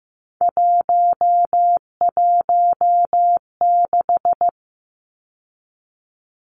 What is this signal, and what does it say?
Morse code, heard as a keyed tone, about 15 wpm, 700 Hz. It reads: 116
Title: Numbers - 3-Digit 15wpm